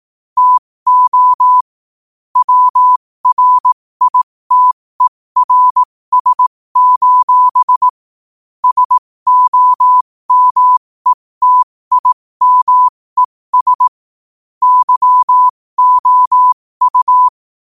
Morse code, heard as a keyed tone, about 18 wpm, spaced as for 15 wpm, 1000 Hz. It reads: TO WRITERS: SOMETIMES YOU
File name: qotdfiles / 15wpm